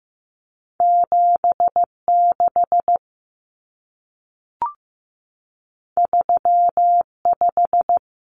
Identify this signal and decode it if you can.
Morse code, heard as a keyed tone, about 15 wpm, 700 Hz. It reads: 7 6 3 5
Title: Common Cut Numbers - 2-Digit 15wpm